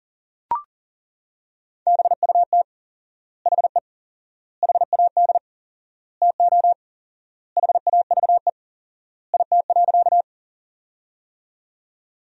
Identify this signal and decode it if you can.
Morse code, heard as a keyed tone, about 40 wpm, 700 Hz. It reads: BUT HE HAD TO HAVE IT.
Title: 4x Word-Spacing - Sentences from Top 100 Words - 40wpm